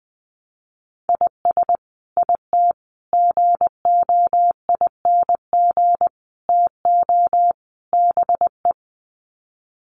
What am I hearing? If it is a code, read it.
Morse code, heard as a keyed tone, about 20 wpm, 700 Hz. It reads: IS IT GOING TO BE
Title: Sets of 5 Words Encore2 20wpm